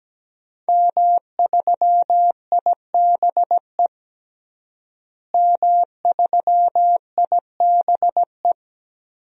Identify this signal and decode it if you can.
Morse code, heard as a keyed tone, about 17 wpm, 700 Hz. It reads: M3IBE M3IBE
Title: DXCC-100 Call Signs - 5 Characters - Repeated 3x - 17wpm